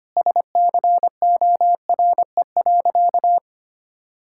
Morse code, heard as a keyed tone, about 25 wpm, 700 Hz. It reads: SCORE.
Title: Sentences from Top 1600 Words 25wpm